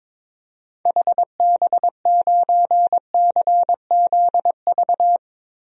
Morse code, heard as a keyed tone, about 22 wpm, 700 Hz. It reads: HB9CZV
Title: DXCC-100 Call Signs - 6 Characters - Repeated 3x - 22wpm